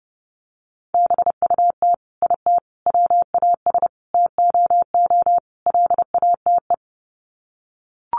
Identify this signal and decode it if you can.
Morse code, heard as a keyed tone, about 30 wpm, 700 Hz. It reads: BUT IT WAS TOO LATE
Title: Sets of 5 Words Encore2 30wpm